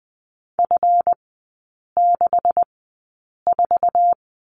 Morse code, heard as a keyed tone, about 20 wpm, spaced as for 8 wpm, 700 Hz. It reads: F64
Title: 138 Lesson 23e - Letter f - Three Characters 20wpm